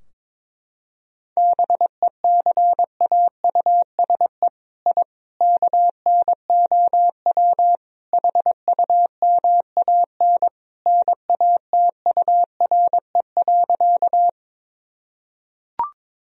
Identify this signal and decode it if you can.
Morse code, heard as a keyed tone, about 22 wpm, 700 Hz. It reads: BECAUSE I KNOW HUMAN NATURE.
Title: Sentences from Top 1000 Words - Review Part 3 - 22wpm